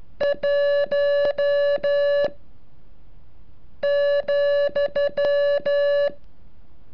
Morse code, heard as a keyed tone, about 14 wpm, 570 Hz. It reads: 1 ,
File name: curso / ejercicio 36